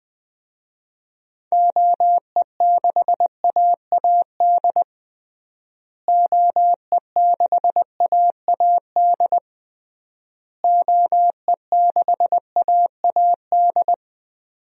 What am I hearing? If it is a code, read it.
Morse code, heard as a keyed tone, about 21 wpm, 700 Hz. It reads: OE6AAD OE6AAD OE6AAD
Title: DXCC-100 Call Signs - 6 Characters - Repeated 3x - 20wpm